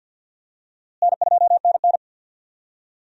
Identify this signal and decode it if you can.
Morse code, heard as a keyed tone, about 52 wpm, 700 Hz. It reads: N2NN
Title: USA Call Signs - 4 Characters - Repeated 3x - 50wpm